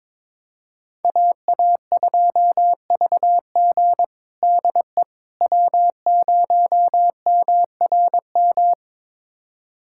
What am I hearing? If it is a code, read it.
Morse code, heard as a keyed tone, about 22 wpm, 700 Hz. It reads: AA2VG DE W0MRM
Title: USA Callsign DE Callsign - 5 Characters - 22wpm